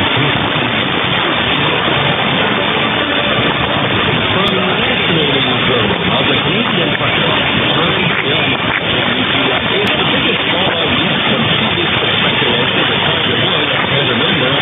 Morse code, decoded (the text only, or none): none